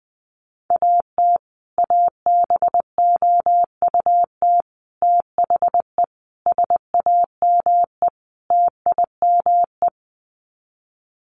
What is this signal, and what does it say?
Morse code, heard as a keyed tone, about 19 wpm, 700 Hz. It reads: AT ABOUT THE SAME TIME
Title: Sets of 5 Words Encore 20wpm